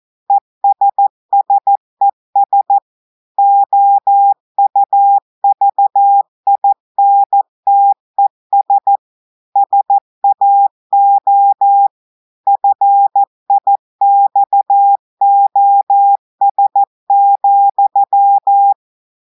Morse code, heard as a keyed tone, about 14 wpm, 800 Hz. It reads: ESSES OUVINTES SAO FIXOS,
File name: audios / portugais / portugais texte en clair 2 vitesse 14 mots minutes